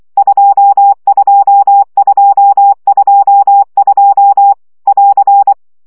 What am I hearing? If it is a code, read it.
Morse code, heard as a keyed tone, about 24 wpm, 800 Hz. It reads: 22222 <AR>
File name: CW gyakorlas 2 20 baud